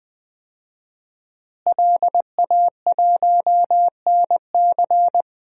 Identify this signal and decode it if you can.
Morse code, heard as a keyed tone, about 20 wpm, 700 Hz. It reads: LA1NC
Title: DXCC-100 Call Signs - 5 Characters - Repeated 2x - 20wpm